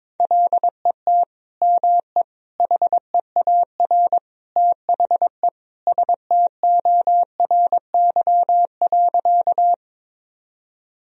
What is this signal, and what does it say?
Morse code, heard as a keyed tone, about 22 wpm, 700 Hz. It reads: LET ME HEAR THE STORY.
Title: Sentences from Top 1000 Words - Review Part 1 - 22wpm